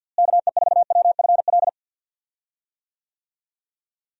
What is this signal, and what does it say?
Morse code, heard as a keyed tone, about 46 wpm, 700 Hz. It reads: KE4WFL
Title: USA Call Signs - 6 Characters - Repeated 3x - 50wpm